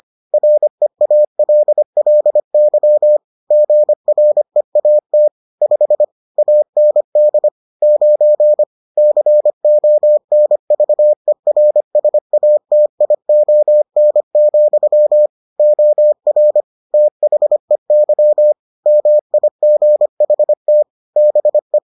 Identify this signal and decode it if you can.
Morse code, heard as a keyed tone, about 25 wpm, 600 Hz. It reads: REALLY GREAT 5 AND 9 CONVERSATION, OR THEY MIGHT BE